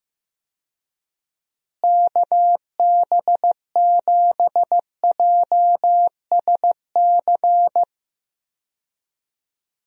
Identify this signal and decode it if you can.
Morse code, heard as a keyed tone, about 15 wpm, 700 Hz. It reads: KB7JSC
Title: USA Call Signs - 6 Characters - Repeated 2x - 15wpm